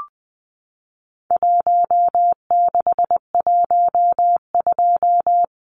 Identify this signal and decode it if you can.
Morse code, heard as a keyed tone, about 19 wpm, 700 Hz. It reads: 1612
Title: Numbers - Contest Style - 4-Digit 20wpm